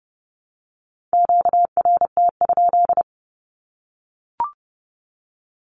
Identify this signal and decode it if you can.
Morse code, heard as a keyed tone, about 30 wpm, 700 Hz. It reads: QRT?
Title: Q-Codes Repeated 3x 30wpm